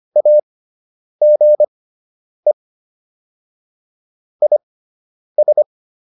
Morse code, heard as a keyed tone, about 25 wpm, spaced as for 9 wpm, 600 Hz. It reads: AGE IS